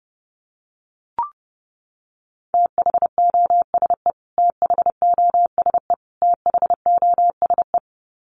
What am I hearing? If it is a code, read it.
Morse code, heard as a keyed tone, about 30 wpm, 700 Hz. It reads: THOSE THOSE THOSE
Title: Top 300 Words Repeated 30wpm